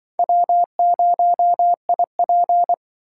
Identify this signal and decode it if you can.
Morse code, heard as a keyed tone, about 25 wpm, 700 Hz. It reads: W0IP